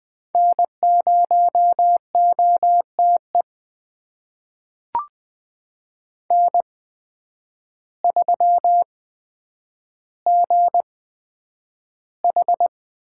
Morse code, heard as a keyed tone, about 20 wpm, spaced as for 9 wpm, 700 Hz. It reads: N0OTE N3GH
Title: USA Call Signs - 5 Characters - 8x Character Spacing - 20wpm